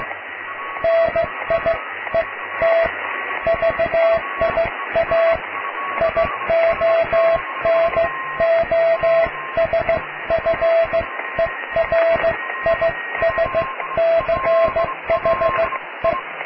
Morse code, heard as a keyed tone, about 16 wpm, spaced as for 13 wpm, 660 Hz, with radio noise. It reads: NIET VIA IONOSFERISCHE
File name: cw / tekst(a)+QRM